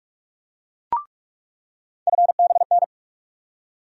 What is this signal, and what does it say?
Morse code, heard as a keyed tone, about 46 wpm, 700 Hz. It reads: FBN
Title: USA Call Sign Postfixes - 3 Letters 45wpm